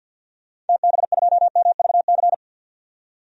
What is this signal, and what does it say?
Morse code, heard as a keyed tone, about 47 wpm, 700 Hz. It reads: TB2MVC
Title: DXCC-100 Call Signs - 6 Characters - 50wpm